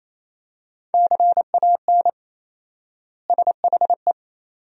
Morse code, heard as a keyed tone, about 28 wpm, 700 Hz. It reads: CAN SHE
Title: 4x Word-Spacing - Sentences from Top 400 Words - 28wpm